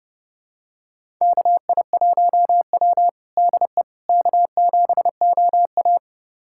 Morse code, heard as a keyed tone, about 30 wpm, 700 Hz. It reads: KI1W DE K7OA
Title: USA Callsign DE Callsign - 4 Characters - 30wpm